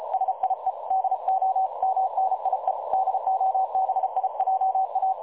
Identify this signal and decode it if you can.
Morse code, heard as a keyed tone, about 33 wpm, 760 Hz, with noise. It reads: SIE DO MNIE DOBIJA
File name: dlazaawansowanych / nagraniazpasma / websdr recording 3565.0kHz